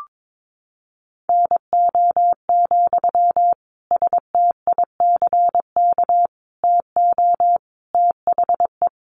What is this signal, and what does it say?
Morse code, heard as a keyed tone, about 22 wpm, 700 Hz. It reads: NO, STICK TO THE
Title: Sentences from Top 1400 Words 22wpm